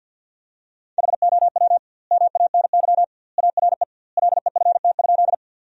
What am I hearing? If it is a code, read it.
Morse code, heard as a keyed tone, about 48 wpm, 700 Hz. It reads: HOW MANY ARE LEFT?
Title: 2x Word-Spacing - Sentences from Top 300 Words - 50wpm